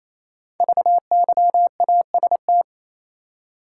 Morse code, heard as a keyed tone, about 27 wpm, 690 Hz. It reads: VYAST
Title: Code Groups - 5 Letters - 28wpm